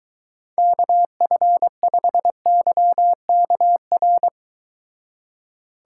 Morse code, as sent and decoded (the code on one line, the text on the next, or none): -.- ..-. ..... -.-- -.- .-.
KF5YKR